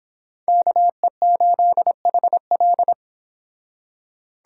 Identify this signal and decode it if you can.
Morse code, heard as a keyed tone, about 26 wpm, 700 Hz. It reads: KE8HL